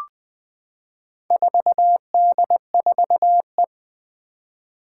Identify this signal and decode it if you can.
Morse code, heard as a keyed tone, about 20 wpm, 700 Hz. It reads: 4D4E
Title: Four Challenging Letter-Number Combinations 20wpm